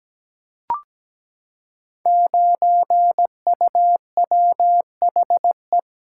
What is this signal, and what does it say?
Morse code, heard as a keyed tone, about 17 wpm, 700 Hz. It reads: 9UWHE
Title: Code Groups - 5 Letter:Numbers - 17wpm